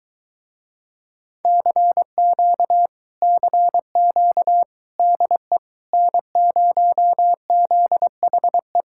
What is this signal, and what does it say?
Morse code, heard as a keyed tone, about 23 wpm, 700 Hz. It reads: CQ CQ DE N0ZHE